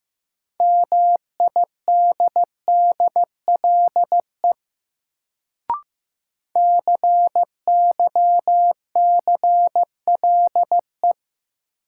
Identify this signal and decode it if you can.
Morse code, heard as a keyed tone, about 15 wpm, 700 Hz. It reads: MIDDLE CYCLE
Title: Sentences from Top 900 Words - New Word Drill - 15wpm